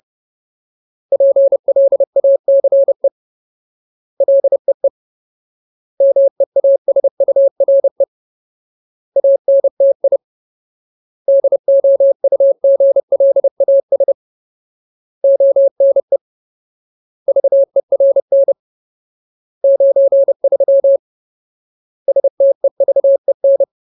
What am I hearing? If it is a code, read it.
Morse code, heard as a keyed tone, about 30 wpm, 550 Hz. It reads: PLACE LEE MEASURE ANTI DOUGLAS ONE VERN 93 STEVEN
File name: Ultimate Word List / 30WPM Ultimate